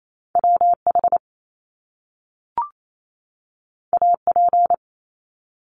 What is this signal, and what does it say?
Morse code, heard as a keyed tone, about 28 wpm, 700 Hz. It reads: WH AP
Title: Top Two Letter Combinations in Words 28wpm